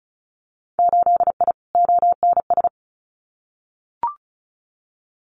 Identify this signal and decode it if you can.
Morse code, heard as a keyed tone, about 36 wpm, 700 Hz. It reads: 8I ONS
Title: Winter Field Day Exchanges Encore 35wpm